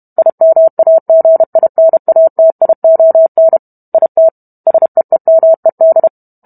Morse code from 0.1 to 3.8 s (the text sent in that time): IMAGINATION